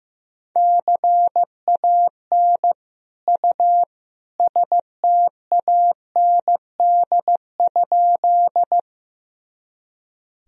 Morse code, heard as a keyed tone, about 15 wpm, 700 Hz. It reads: CAN U STAND?
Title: Sentences from Top 300 Words and Abbreviations 15wpm